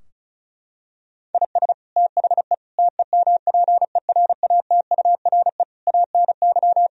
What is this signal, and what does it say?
Morse code, heard as a keyed tone, about 34 wpm, 700 Hz. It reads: IS THE TEMPERATURE ANY
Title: Sentences from Top 1000 Words - Review Part 4 - 35wpm